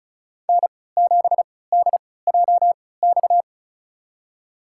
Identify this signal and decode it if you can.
Morse code, heard as a keyed tone, about 34 wpm, spaced as for 20 wpm, 700 Hz. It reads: N7DJX
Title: USA Call Signs - 5 Characters - 3x Character Spacing - 35wpm